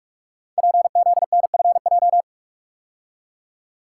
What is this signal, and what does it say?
Morse code, heard as a keyed tone, about 44 wpm, 700 Hz. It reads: PZNFJ
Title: Code Groups - 5 Letters - 45wpm